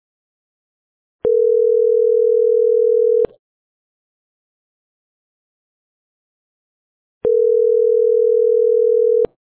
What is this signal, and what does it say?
Ringback tone (North America), 440 + 480 Hz.